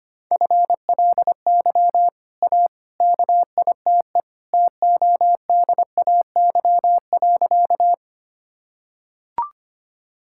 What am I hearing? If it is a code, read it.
Morse code, heard as a keyed tone, about 25 wpm, 700 Hz. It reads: FLY A KITE TODAY.
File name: Easy Sentences with Hints - 25wpm